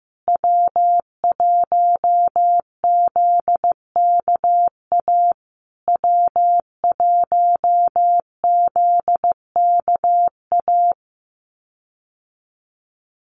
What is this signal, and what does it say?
Morse code, heard as a keyed tone, about 15 wpm, 700 Hz. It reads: W1ZKA W1ZKA
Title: USA Call Signs Repeated 3x 15wpm